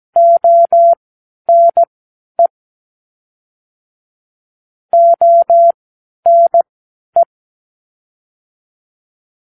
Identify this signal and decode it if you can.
Morse code, heard as a keyed tone, about 17 wpm, spaced as for 11 wpm, 680 Hz. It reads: ONE ONE
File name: k7qo / Lesson 40 (1)